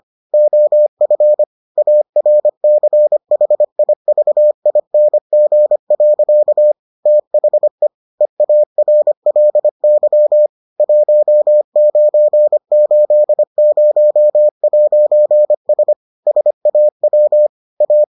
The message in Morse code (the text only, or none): OF ARCHIVING. THE EARLY 1980'S SAW A